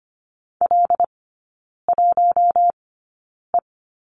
Morse code, heard as a keyed tone, about 25 wpm, spaced as for 9 wpm, 700 Hz. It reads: L1E
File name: course / 25:10wpm / 093 Lesson 16e - Letter c - Three Characters 25wpm